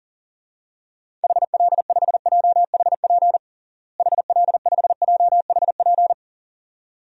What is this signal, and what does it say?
Morse code, heard as a keyed tone, about 40 wpm, 700 Hz. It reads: HL5JHP HL5JHP
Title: DXCC-100 Call Signs - 6 Characters - Repeated 2x - 40wpm